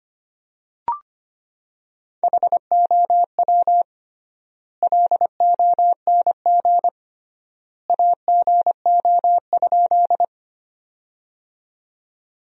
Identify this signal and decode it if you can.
Morse code, heard as a keyed tone, about 25 wpm, 700 Hz. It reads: HOW LONG AGO?
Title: 3x Word-Spacing - Sentences from Top 500 Words - 25wpm